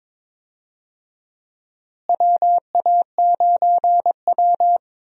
Morse code, heard as a keyed tone, about 22 wpm, 700 Hz. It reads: WA9W
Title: USA Call Signs - 4 Characters - Repeated 3x - 22wpm